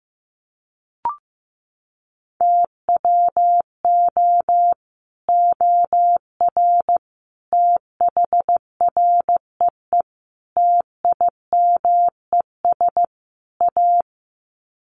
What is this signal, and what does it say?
Morse code, heard as a keyed tone, about 15 wpm, 700 Hz. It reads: TWO OR THREE TIMES A
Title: Sets of 5 Words Encore2 15wpm